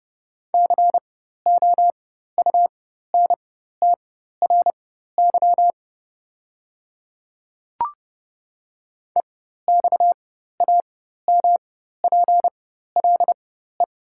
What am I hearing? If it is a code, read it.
Morse code, heard as a keyed tone, about 30 wpm, spaced as for 14 wpm, 700 Hz. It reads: COUNTRY EXAMPLE
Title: 4x Character Spacing - Top 300 Words - 30wpm